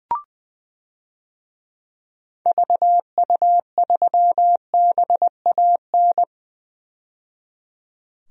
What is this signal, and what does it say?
Morse code, heard as a keyed tone, about 20 wpm, 700 Hz. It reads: VU3BAN
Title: DXCC-100 Call Signs - 6 Characters - 20wpm